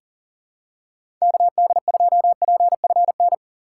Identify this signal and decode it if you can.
Morse code, heard as a keyed tone, about 40 wpm, 700 Hz. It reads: KD2PFN
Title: USA Call Signs - 6 Characters - 3x Character Spacing - 40wpm